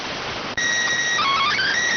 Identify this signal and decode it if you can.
SSTV (slow-scan television), PD 290.